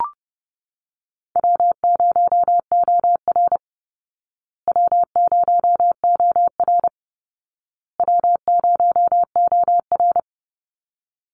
Morse code, heard as a keyed tone, about 30 wpm, 700 Hz. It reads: W0OR W0OR W0OR
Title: Easy Call Signs Repeated 3x - 30wpm